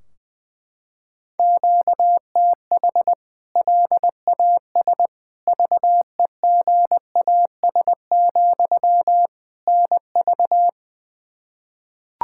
Morse code, heard as a keyed tone, about 21 wpm, 700 Hz. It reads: QTH LAS VEGAS, NV
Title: K1USN SST QSO Practice 20wpm